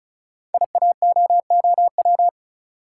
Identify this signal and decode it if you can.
Morse code, heard as a keyed tone, about 36 wpm, 700 Hz. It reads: IAOOW
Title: Code Groups - 5 Letters - 35wpm